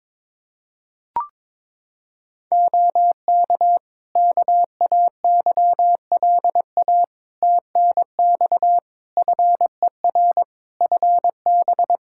Morse code, heard as a keyed tone, about 22 wpm, 700 Hz. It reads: OK KAYLA TNX FER FB
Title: K1USN SST QSO Practice 22wpm